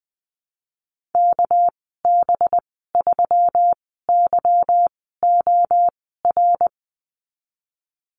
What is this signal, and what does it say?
Morse code, heard as a keyed tone, about 20 wpm, spaced as for 15 wpm, 700 Hz. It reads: KB3YOR
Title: 231 Lesson 37g - Letter x - Call Signs 20wpm